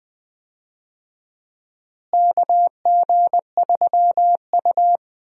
Morse code, heard as a keyed tone, about 20 wpm, 700 Hz. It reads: KG3U
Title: USA Call Signs - 4 Characters - Repeated 2x - 20wpm